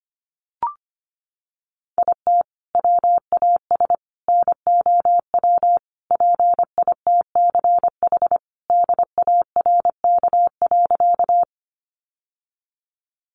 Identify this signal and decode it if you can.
Morse code, heard as a keyed tone, about 25 wpm, 700 Hz. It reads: IT WAS NOW PITCH DARK.
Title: Sentences from Top 1300 Words 25wpm